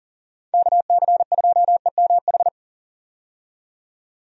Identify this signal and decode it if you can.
Morse code, heard as a keyed tone, about 37 wpm, 700 Hz. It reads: KC2EMH
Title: USA Call Signs - 6 Characters - 40wpm